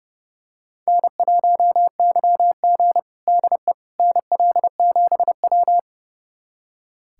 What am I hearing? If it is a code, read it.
Morse code, heard as a keyed tone, about 32 wpm, 700 Hz. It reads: N1YG DE NL7W
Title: USA Callsign DE Callsign - 4 Characters - 30wpm